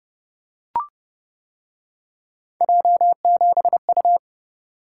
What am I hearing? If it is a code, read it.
Morse code, heard as a keyed tone, about 31 wpm, 700 Hz. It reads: J7U